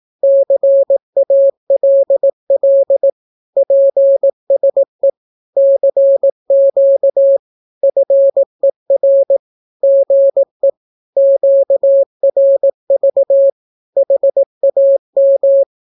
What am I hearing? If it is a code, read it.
Morse code, heard as a keyed tone, about 18 wpm, 550 Hz. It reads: CALL PSE CQ FER GE QRV HAM
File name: naukapodstaw / skroty / skroty